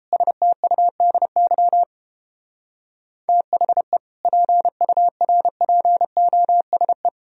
Morse code, heard as a keyed tone, about 29 wpm, 700 Hz. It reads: STUDY THE PURPOSE